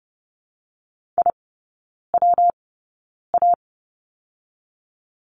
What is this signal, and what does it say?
Morse code, heard as a keyed tone, about 30 wpm, 700 Hz. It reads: I W A
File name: Lesson 19e - Letter w - Three Characters 30wpm